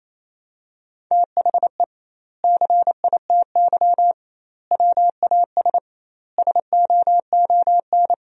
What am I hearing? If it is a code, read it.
Morse code, heard as a keyed tone, about 28 wpm, 700 Hz. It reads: THE CITY WAS SOON
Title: 2x Word-Spacing - Sentences from Top 500 Words - 28wpm